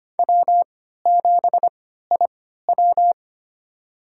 Morse code, heard as a keyed tone, about 25 wpm, spaced as for 14 wpm, 700 Hz. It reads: W7IW
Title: USA Call Signs - 3x Character Spacing - 25wpm